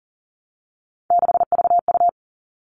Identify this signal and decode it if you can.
Morse code, heard as a keyed tone, about 40 wpm, 700 Hz. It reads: 6VU
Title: All Cut Numbers - 3-Digit 40wpm